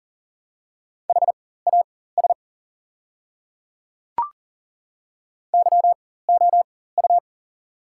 Morse code, heard as a keyed tone, about 37 wpm, spaced as for 18 wpm, 700 Hz. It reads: HAS YOU
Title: 4x Character Spacing - Top 100 Words - 40wpm